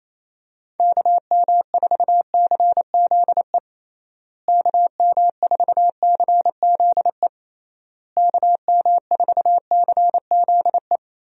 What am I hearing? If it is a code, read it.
Morse code, heard as a keyed tone, about 28 wpm, 700 Hz. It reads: KM4CZE KM4CZE KM4CZE
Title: USA Call Signs - 6 Characters - Repeated 3x - 28wpm